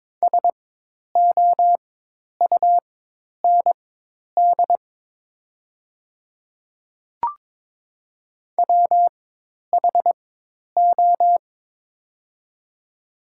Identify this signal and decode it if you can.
Morse code, heard as a keyed tone, about 22 wpm, spaced as for 10 wpm, 700 Hz. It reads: SOUND WHO
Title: 4x Character Spacing - Top 100 Words - 22wpm